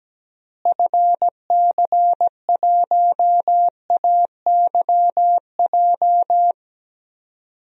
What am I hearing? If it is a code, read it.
Morse code, heard as a keyed tone, about 17 wpm, 700 Hz. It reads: FC1AYJ
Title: DXCC-100 Call Signs - 6 Characters - Repeated 3x - 17wpm